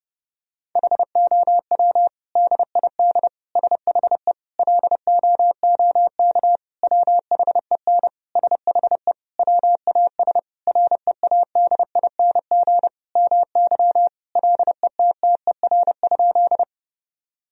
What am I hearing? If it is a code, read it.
Morse code, heard as a keyed tone, about 30 wpm, 700 Hz. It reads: HOW DID SHE LOOK WHEN SHE WAS READING MY LETTER?
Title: Sentences from Top 1000 Words - Review Part 3 - 30wpm